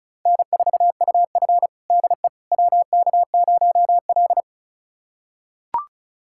Morse code, heard as a keyed tone, about 36 wpm, 700 Hz. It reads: N4UF DE WK0L
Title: USA Callsign DE Callsign - 4 Characters - 35wpm